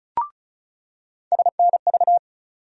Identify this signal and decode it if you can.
Morse code, heard as a keyed tone, about 36 wpm, 700 Hz. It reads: SNV
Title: USA Call Sign Postfixes - 3 Letters 35wpm